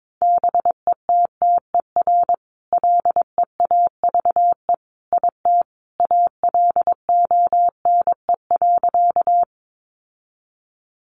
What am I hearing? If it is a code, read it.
Morse code, heard as a keyed tone, about 22 wpm, 700 Hz. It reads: BETTER LEAVE IT ALONE.
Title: Sentences from Top 1000 Words 22wpm